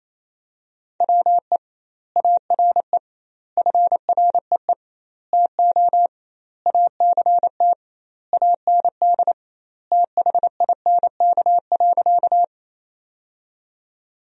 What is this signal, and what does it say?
Morse code, heard as a keyed tone, about 28 wpm, 700 Hz. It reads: WE ARE FREE TO ACT AND THINK.
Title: 2x Word-Spacing - Sentences from Top 500 Words - 28wpm